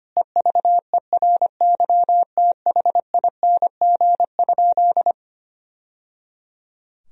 Morse code, heard as a keyed tone, about 25 wpm, 700 Hz. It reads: EVERYTHING?
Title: Sentences from Top 1000 Words - Review Part 2 - 25wpm